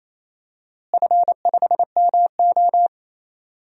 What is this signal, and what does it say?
Morse code, heard as a keyed tone, about 28 wpm, 700 Hz. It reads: F5MO
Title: DXCC-100 Call Signs - 4 Characters - Repeated 2x - 28wpm